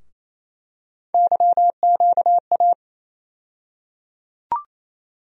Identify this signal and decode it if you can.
Morse code, heard as a keyed tone, about 28 wpm, 700 Hz. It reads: YQA
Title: USA Call Sign Postfixes - 3 Letters 28wpm